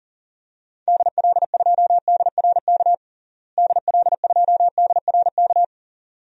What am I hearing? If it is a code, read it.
Morse code, heard as a keyed tone, about 40 wpm, 700 Hz. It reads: DL2DRK DL2DRK
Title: DXCC-100 Call Signs - 6 Characters - Repeated 2x - 40wpm